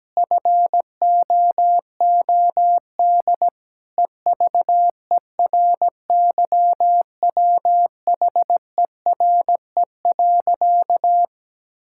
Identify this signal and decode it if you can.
Morse code, heard as a keyed tone, about 17 wpm, 700 Hz. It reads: FOOD EVERYWHERE.
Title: Sentences from Top 1600 Words 17wpm